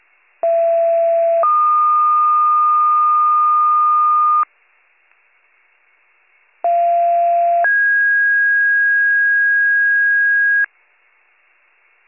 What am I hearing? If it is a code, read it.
Two-tone page, heard as a one-second tone, then a three-second tone, about 670 then 1220 Hz, about 690 then 1650 Hz. It is heard from 0.4 to 10.7 s.